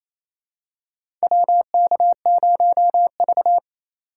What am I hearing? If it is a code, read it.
Morse code, heard as a keyed tone, about 27 wpm, 700 Hz. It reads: WK0V
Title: USA Call Signs - 4 Characters - 4x Character Spacing - 28wpm